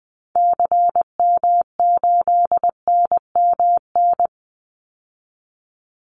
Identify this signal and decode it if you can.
Morse code, heard as a keyed tone, about 20 wpm, 700 Hz. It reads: CM8NMN
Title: Call Signs Repeated 3x - 20wpm